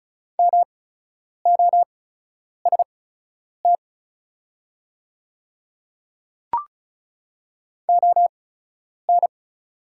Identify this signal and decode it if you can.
Morse code, heard as a keyed tone, about 34 wpm, spaced as for 10 wpm, 690 Hz. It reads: MOST ON
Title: 8x Character Spacing - Top 200 Words - 35wpm